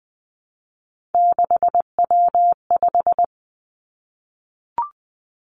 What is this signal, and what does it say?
Morse code, heard as a keyed tone, about 20 wpm, 700 Hz. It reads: 6W5
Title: Three Challenging Letter-Number Combinations 20wpm